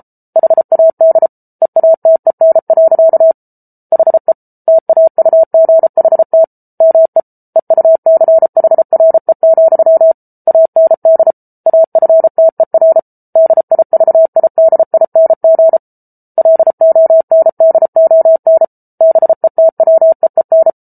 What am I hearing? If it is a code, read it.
Morse code, heard as a keyed tone, about 34 wpm, 650 Hz. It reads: HAD EATEN. HE TAUGHT ME EUCHRE, AND AFTER DIVIDING LONDON BETWEEN